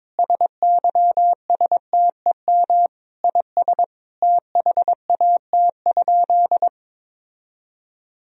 Morse code, heard as a keyed tone, about 22 wpm, 700 Hz. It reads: SYSTEM IS THAT?
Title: Sentences from Top 1000 Words - Review Part 3 - 22wpm